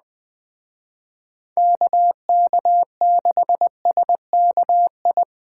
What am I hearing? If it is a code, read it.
Morse code, heard as a keyed tone, about 20 wpm, 700 Hz. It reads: KK6SKI
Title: Speed Racing - USA Call Signs 17wpm